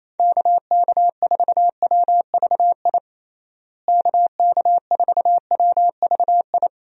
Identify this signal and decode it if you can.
Morse code, heard as a keyed tone, about 28 wpm, 700 Hz. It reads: KK4WVI KK4WVI
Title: USA Call Signs - 6 Characters - Repeated 3x - 28wpm